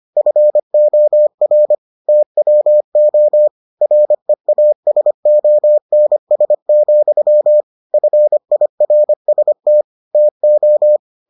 Morse code, heard as a keyed tone, about 25 wpm, 600 Hz. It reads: FOR TWO REASONS, FIRST TO